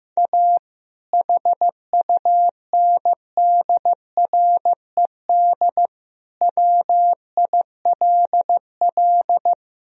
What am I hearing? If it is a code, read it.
Morse code, heard as a keyed tone, about 16 wpm, 700 Hz. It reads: A HUNDRED WILL
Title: Sentences from Top 1000 Words - Review Part 2 - 15wpm